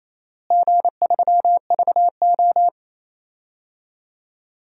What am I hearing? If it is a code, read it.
Morse code, heard as a keyed tone, about 28 wpm, 700 Hz. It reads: G3VO
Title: DXCC-100 Call Signs - 4 Characters - Repeated 3x - 28wpm